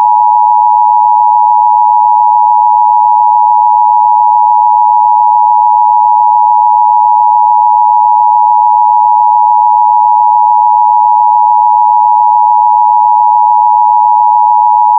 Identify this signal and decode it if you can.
Emergency Alert System (SAME) tones (the 853 and 960 Hz attention tone). It is heard from 0.9 to 14.0 s.